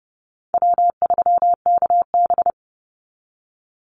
Morse code, heard as a keyed tone, about 30 wpm, 700 Hz. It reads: W3KB
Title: Easy Call Signs - 30wpm